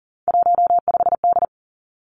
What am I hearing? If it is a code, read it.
Morse code, heard as a keyed tone, about 38 wpm, 700 Hz. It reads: 15D
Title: Three Challenging Letter-Number Combinations 40wpm